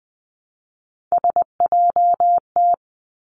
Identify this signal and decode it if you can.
Morse code, heard as a keyed tone, about 20 wpm, 700 Hz. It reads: SJT
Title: Three Challenging Letter-Number Combinations 20wpm